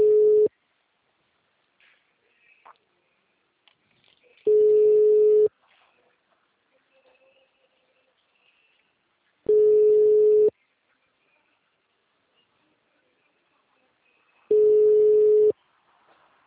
Ringback tone (Europe, 425 Hz).
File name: out- -2009- - - .9298